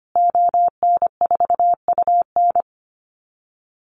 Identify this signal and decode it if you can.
Morse code, heard as a keyed tone, about 25 wpm, 700 Hz. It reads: ON4UN
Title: Call Signs Repeated 3x - 25wpm